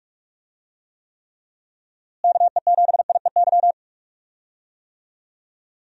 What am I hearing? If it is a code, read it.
Morse code, heard as a keyed tone, about 46 wpm, 700 Hz. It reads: KE7IEY
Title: USA Call Signs - 6 Characters - Repeated 3x - 45wpm